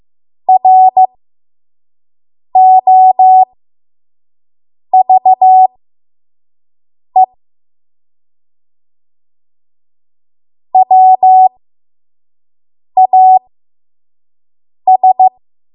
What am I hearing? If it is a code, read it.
Morse code, heard as a keyed tone, about 15 wpm, spaced as for 5 wpm, 750 Hz. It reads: ROVE WAS
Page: 5 WPM Code Archive